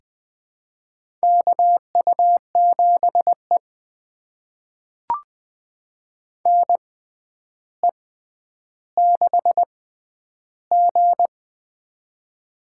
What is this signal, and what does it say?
Morse code, heard as a keyed tone, about 20 wpm, spaced as for 10 wpm, 700 Hz. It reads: KU7E NE6G
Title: USA Call Signs - 4 Characters - 6x Character Spacing - 20wpm